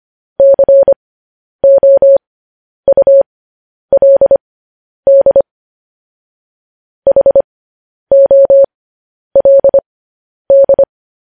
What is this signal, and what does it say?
Morse code, heard as a keyed tone, about 25 wpm, spaced as for 10 wpm, 550 Hz. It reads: COULD HOLD
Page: News Headlines in Morse Code at 10 WPM